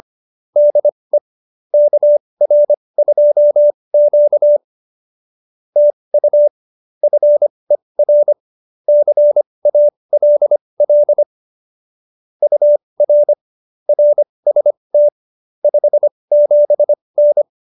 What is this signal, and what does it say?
Morse code, heard as a keyed tone, about 25 wpm, spaced as for 20 wpm, 590 Hz. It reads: DE KR2Q TU FER CALL UR RST 57N